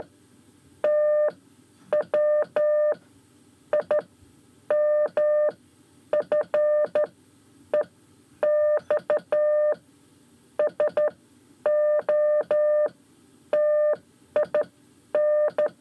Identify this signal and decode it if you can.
Morse code, heard as a keyed tone, about 12 wpm, spaced as for 9 wpm, 600 Hz. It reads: TW IMFEX SOTIN
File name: 5 Lezione CW 1